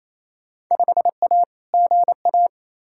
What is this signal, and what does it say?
Morse code, heard as a keyed tone, about 28 wpm, 700 Hz. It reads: 5A GA
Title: Field Day Exchanges Encore - 28wpm